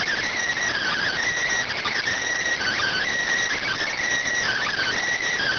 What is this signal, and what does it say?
Emergency Alert System (SAME) tones (header bursts), heard as harsh buzzing bursts. It is heard from 0.7 to 5.0 s.